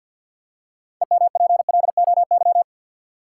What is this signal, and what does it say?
Morse code, heard as a keyed tone, about 53 wpm, 700 Hz. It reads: EMPLOY
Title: Six Letter Words - 50wpm